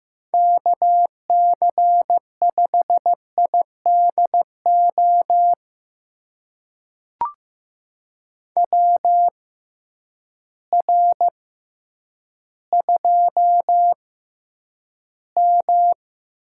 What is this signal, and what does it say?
Morse code, heard as a keyed tone, about 15 wpm, spaced as for 9 wpm, 700 Hz. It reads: KC5IDO WR2M
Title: USA Call Signs - 6x Character Spacing - 15wpm